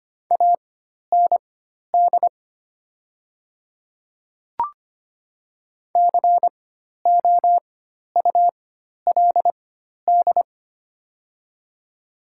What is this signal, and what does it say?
Morse code, heard as a keyed tone, about 25 wpm, spaced as for 12 wpm, 700 Hz. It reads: AND COULD
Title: 4x Character Spacing - Top 100 Words - 25wpm